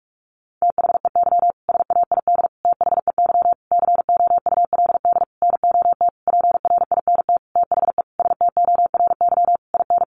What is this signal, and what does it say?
Morse code, heard as a keyed tone, about 44 wpm, 700 Hz. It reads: THEY SAID THEY COULD NOT PRINT THE STORY IN
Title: Sentences from Top 1400 Words 45wpm